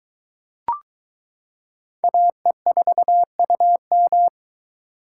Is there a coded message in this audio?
Yes: Morse code from 2.0 to 4.3 s, heard as a keyed tone, about 21 wpm, 700 Hz. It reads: AE4UM